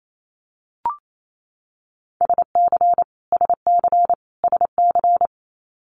Morse code, heard as a keyed tone, about 28 wpm, 700 Hz. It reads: SC SC SC
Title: Multi-letter Phonemes Repeated 3x 28wpm